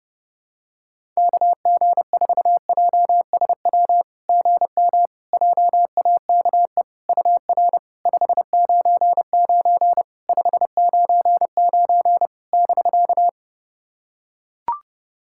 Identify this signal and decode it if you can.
Morse code, heard as a keyed tone, about 30 wpm, 700 Hz. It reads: KG4JSW GM JAKE UR 599 599 <BK>
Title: SOTA QSO Practice 30wpm